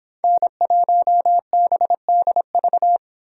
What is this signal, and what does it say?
Morse code, heard as a keyed tone, about 26 wpm, 700 Hz. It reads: N1BDV